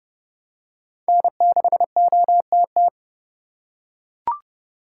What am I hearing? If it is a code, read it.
Morse code, heard as a keyed tone, about 30 wpm, 700 Hz. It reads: N6OTT
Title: USA Call Signs - 4x Character Spacing - 30wpm